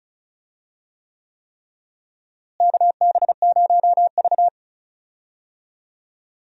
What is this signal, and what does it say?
Morse code, heard as a keyed tone, about 36 wpm, 700 Hz. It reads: KB0V
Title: USA Call Signs - 4 Characters - Repeated 3x - 35wpm